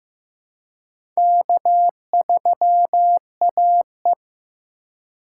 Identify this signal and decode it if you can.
Morse code, heard as a keyed tone, about 15 wpm, 700 Hz. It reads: K3AE
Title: USA Call Signs - 4 Characters - 4x Character Spacing - 15wpm